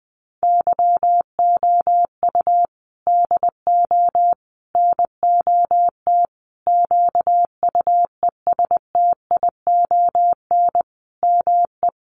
Morse code, heard as a keyed tone, about 20 wpm, 700 Hz. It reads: YOU DO NOT QUESTION ME
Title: Sentences from Top 1100 Words 20wpm